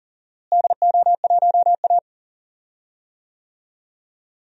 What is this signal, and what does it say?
Morse code, heard as a keyed tone, about 40 wpm, 700 Hz. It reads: DO1A
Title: DXCC-100 Call Signs - 4 Characters - Repeated 3x - 40wpm